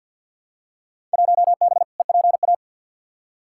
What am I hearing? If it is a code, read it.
Morse code, heard as a keyed tone, about 51 wpm, 700 Hz. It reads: 1B EPA
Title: Field Day Exchanges Encore - 50wpm